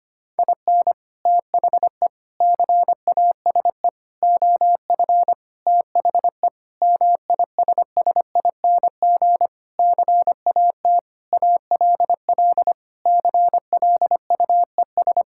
Morse code, heard as a keyed tone, about 25 wpm, 700 Hz. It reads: IN THE CASE OF THE MISSING CAT ALL CLUES
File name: practice / 25wpm / Sentences with 4-Word Phrases - 25wpm